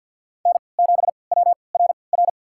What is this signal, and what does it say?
Morse code, heard as a keyed tone, about 53 wpm, spaced as for 29 wpm, 700 Hz. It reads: N7WRR
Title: USA Call Signs - 5 Characters - 3x Character Spacing - 50wpm